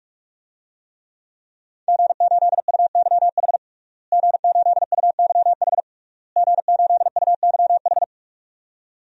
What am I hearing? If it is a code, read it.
Morse code, heard as a keyed tone, about 41 wpm, 700 Hz. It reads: G8UYH G8UYH G8UYH
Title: DXCC-100 Call Signs - 5 Characters - Repeated 3x - 45wpm